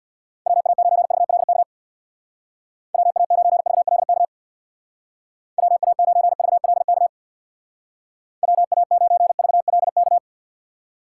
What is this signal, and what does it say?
Morse code, heard as a keyed tone, about 64 wpm, 700 Hz. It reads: WA9VLK WA9VLK WA9VLK WA9VLK